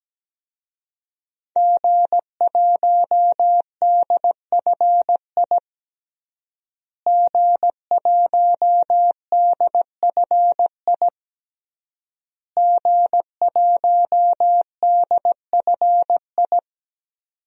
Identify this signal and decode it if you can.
Morse code, heard as a keyed tone, about 17 wpm, 700 Hz. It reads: G1DFI G1DFI G1DFI
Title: DXCC-100 Call Signs - 5 Characters - Repeated 3x - 17wpm